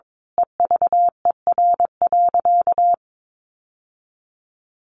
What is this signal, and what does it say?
Morse code, heard as a keyed tone, about 22 wpm, 700 Hz. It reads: EVER.
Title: Sentences from Top 1200 Words 22wpm